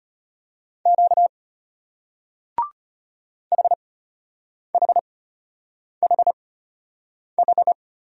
Morse code, heard as a keyed tone, about 33 wpm, spaced as for 8 wpm, 700 Hz. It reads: Q HHHH